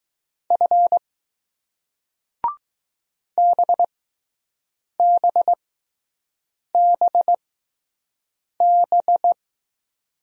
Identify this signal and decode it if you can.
Morse code, heard as a keyed tone, about 20 wpm, spaced as for 6 wpm, 700 Hz. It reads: F BBBB